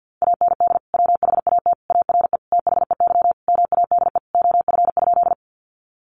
Morse code, heard as a keyed tone, about 50 wpm, 700 Hz. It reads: AND WHAT ARE THEY MADE OF?
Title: Sentences from Top 200 Words 50wpm